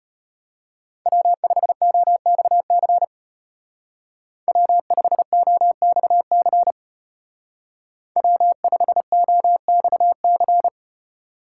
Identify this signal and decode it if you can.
Morse code, heard as a keyed tone, about 34 wpm, 700 Hz. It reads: W5OXC W5OXC W5OXC